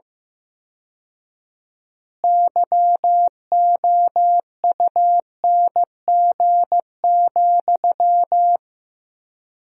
Morse code, heard as a keyed tone, about 15 wpm, 700 Hz. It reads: YOUNG,
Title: 4x Word-Spacing - Sentences from Top 400 Words - 15wpm